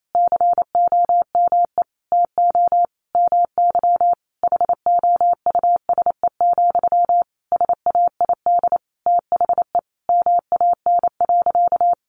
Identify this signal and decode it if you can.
Morse code, heard as a keyed tone, about 28 wpm, 700 Hz. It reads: COME TO MY HOUSE, SAID THE MAN.
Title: Sentences from Top 400 Words 28wpm